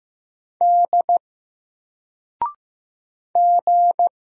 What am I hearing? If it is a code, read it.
Morse code, heard as a keyed tone, about 15 wpm, 700 Hz. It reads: D G
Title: Single Letter-Number 15wpm